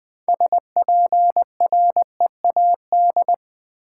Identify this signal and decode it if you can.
Morse code, heard as a keyed tone, about 20 wpm, 700 Hz. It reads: SPREAD
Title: 6 Letter Words - First 3 Letters Given - 20wpm